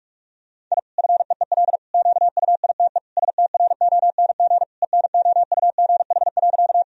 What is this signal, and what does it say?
Morse code, heard as a keyed tone, about 46 wpm, 700 Hz. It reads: I FEEL QUITE STRONG ENOUGH.
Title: Sentences from Top 1000 Words - Review Part 3 - 45wpm